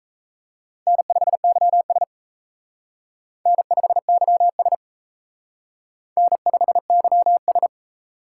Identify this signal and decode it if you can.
Morse code, heard as a keyed tone, about 35 wpm, 700 Hz. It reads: N5YS N5YS N5YS